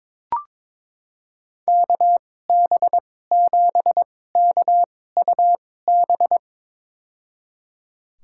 Morse code, heard as a keyed tone, about 22 wpm, spaced as for 16 wpm, 700 Hz. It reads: KB7KUB
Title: USA Call Signs - 6 Characters - 2x Character Spacing - 22wpm